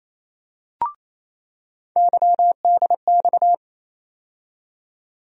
Morse code, heard as a keyed tone, about 28 wpm, 700 Hz. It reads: YDX
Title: USA Call Sign Postfixes - 3 Letters 28wpm